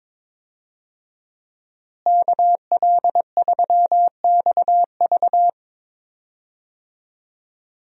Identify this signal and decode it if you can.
Morse code, heard as a keyed tone, about 21 wpm, 700 Hz. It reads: KL3XV
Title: USA Call Signs - 5 Characters - Repeated 3x - 22wpm